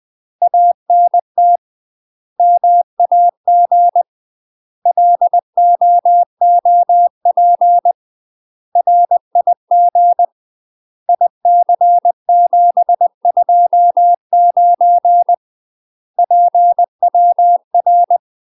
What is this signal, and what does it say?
Morse code, heard as a keyed tone, about 20 wpm, 700 Hz. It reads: ANT MAG LOOP RIG IC729 PWR
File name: UK MASTER QSOs TRAINING / 20WPM / UK MASTER QSOs 0008